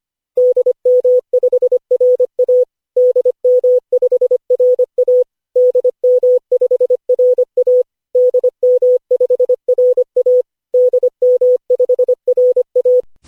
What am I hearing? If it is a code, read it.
Morse code, heard as a keyed tone, about 25 wpm, 500 Hz. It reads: DM5RA DM5RA DM5RA DM5RA DM5RA
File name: cw / Session5 cw training